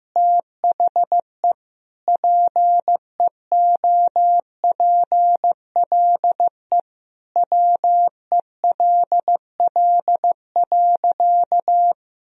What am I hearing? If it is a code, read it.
Morse code, heard as a keyed tone, about 15 wpm, 700 Hz. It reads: THE PEOPLE WELL.
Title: Sentences from Top 1000 Words - Review Part 2 - 15wpm